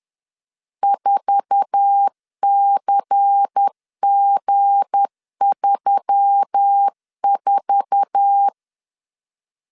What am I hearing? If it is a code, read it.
Morse code, heard as a keyed tone, about 10 wpm, 800 Hz. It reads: 4CG34
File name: LEZIONI / 8 YJ45 / 8 YJ4 10wpm 2633